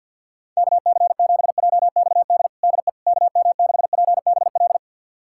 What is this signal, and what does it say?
Morse code, heard as a keyed tone, about 49 wpm, 700 Hz. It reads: KC7JXD DE KM6PBL
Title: USA Callsign DE Callsign - 6 Characters - 50wpm